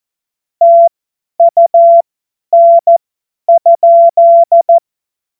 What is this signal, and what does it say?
Morse code, heard as a keyed tone, about 14 wpm, spaced as for 10 wpm, 680 Hz. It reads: TUN?
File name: Morsekurs / Witze / Witz 20. Dezember 2022 50 F70